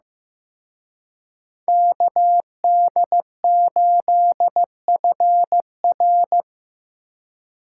Morse code, heard as a keyed tone, about 18 wpm, 700 Hz. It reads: KD8FR
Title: Speed Racing - USA 2x2 Call Signs 15wpm